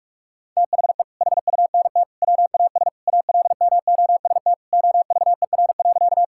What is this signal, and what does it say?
Morse code, heard as a keyed tone, about 46 wpm, 700 Hz. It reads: THE HUNT WAS ALMOST OVER.
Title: Sentences from Top 1600 Words 45wpm